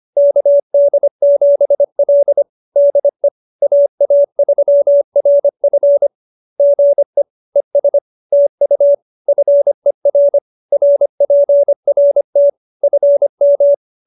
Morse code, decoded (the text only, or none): KD7L DE AA3RF GE ES TU FER RPRT FM